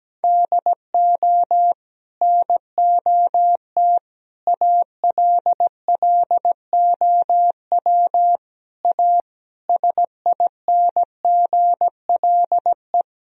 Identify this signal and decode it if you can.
Morse code, heard as a keyed tone, about 17 wpm, 700 Hz. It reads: DO NOT ALLOW A SINGLE
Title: Sentences from Top 1000 Words - Review Part 4 - 17wpm